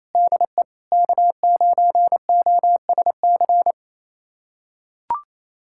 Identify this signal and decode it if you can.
Morse code, heard as a keyed tone, about 28 wpm, 700 Hz. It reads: DE K9OSC
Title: USA Callsign DE Callsign - 5 Characters - 28wpm